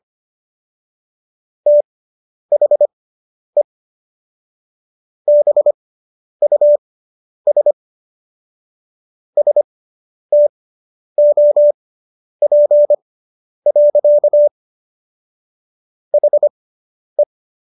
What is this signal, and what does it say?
Morse code, heard as a keyed tone, about 25 wpm, spaced as for 10 wpm, 590 Hz. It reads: THE BUS STOP. HE